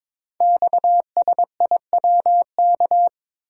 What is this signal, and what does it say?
Morse code, heard as a keyed tone, about 22 wpm, 700 Hz. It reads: XSIWK
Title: Code Groups - 5 Letters - 22wpm